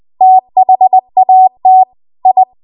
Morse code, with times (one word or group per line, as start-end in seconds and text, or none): 0.2-1.8 THAT
2.3-2.4 I